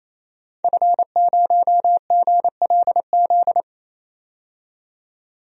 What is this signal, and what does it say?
Morse code, heard as a keyed tone, about 28 wpm, 700 Hz. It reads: F0GLZ
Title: DXCC-100 Call Signs - 5 Characters - 28wpm